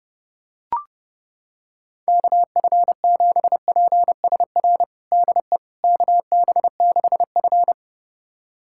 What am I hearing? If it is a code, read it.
Morse code, heard as a keyed tone, about 30 wpm, 700 Hz. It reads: KF7PSR DE KB6F
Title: USA Callsign DE Callsign 30wpm